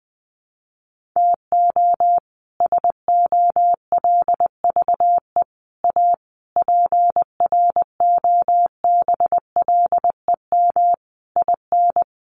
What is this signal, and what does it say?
Morse code, heard as a keyed tone, about 20 wpm, 700 Hz. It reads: TO SOLVE A PROBLEM IN
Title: Sentences from Top 1400 Words 20wpm